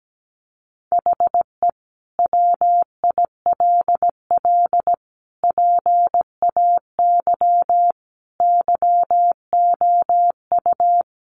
Morse code, heard as a keyed tone, about 17 wpm, 700 Hz. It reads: HE WILL PAY YOU
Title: Sentences from Top 1400 Words 17wpm